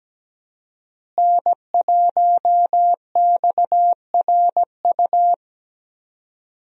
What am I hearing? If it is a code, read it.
Morse code, heard as a keyed tone, about 17 wpm, 700 Hz. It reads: N1XRU
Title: USA Call Signs - 5 Characters - Repeated 2x - 17wpm